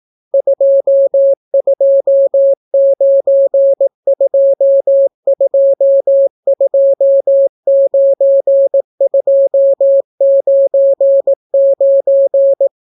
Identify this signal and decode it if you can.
Morse code, heard as a keyed tone, about 18 wpm, 550 Hz. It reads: 2292229299